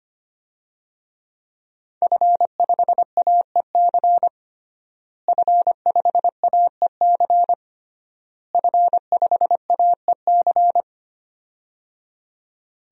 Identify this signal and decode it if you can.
Morse code, heard as a keyed tone, about 25 wpm, 700 Hz. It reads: F5AEC F5AEC F5AEC
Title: DXCC-100 Call Signs - 5 Characters - Repeated 3x - 25wpm